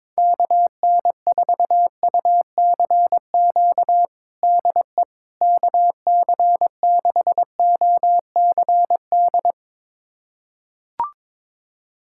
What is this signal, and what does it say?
Morse code, heard as a keyed tone, about 22 wpm, 700 Hz. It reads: KN4UCQ DE KC6OCD
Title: USA Callsign DE Callsign - 6 Characters - 22wpm